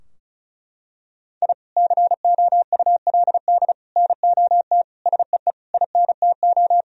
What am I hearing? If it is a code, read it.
Morse code, heard as a keyed tone, about 34 wpm, 700 Hz. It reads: I COULD NOT SEE INTO
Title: Sentences from Top 1000 Words - Review Part 4 - 35wpm